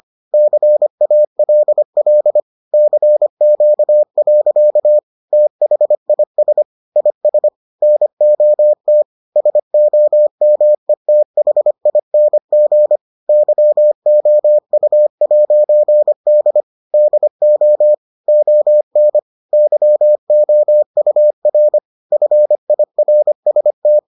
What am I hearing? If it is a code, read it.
Morse code, heard as a keyed tone, about 25 wpm, 600 Hz. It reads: CALL CQ. THIS IS NOT SOMETHING YOU'D DO ON YOUR FIRST